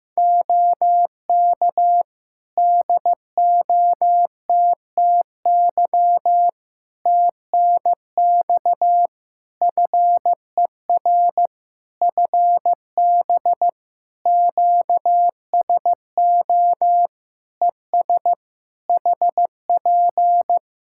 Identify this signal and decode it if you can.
Morse code, heard as a keyed tone, about 15 wpm, 700 Hz. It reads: OK DOTTY TNX FER FB QSO ES HP
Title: K1USN SST QSO Practice 15wpm